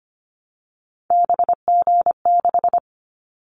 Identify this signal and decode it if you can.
Morse code, heard as a keyed tone, about 25 wpm, 700 Hz. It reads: BG6
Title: Callsign Prefixes for Top 100 DXCC - 25wpm